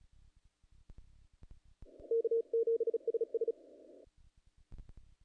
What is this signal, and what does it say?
Morse code, heard as a keyed tone, about 41 wpm, 460 Hz. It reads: K7SS
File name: contests / naqp-cw-k5tr-op-k5pi- -1800